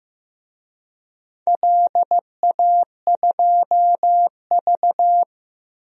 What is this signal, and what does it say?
Morse code, heard as a keyed tone, about 15 wpm, 700 Hz. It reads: LA2V
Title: DXCC-100 Call Signs - 4 Characters - 15wpm